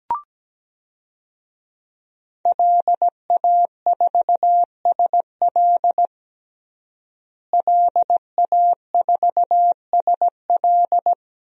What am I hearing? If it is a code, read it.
Morse code, heard as a keyed tone, about 17 wpm, 700 Hz. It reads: LA4SL LA4SL
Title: DXCC-100 Call Signs - 5 Characters - Repeated 3x - 17wpm